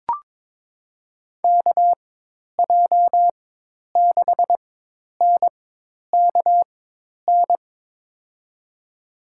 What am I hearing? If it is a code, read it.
Morse code, heard as a keyed tone, about 22 wpm, spaced as for 10 wpm, 700 Hz. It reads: KJ6NKN
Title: USA Call Signs - 6 Characters - 4x Character Spacing - 22wpm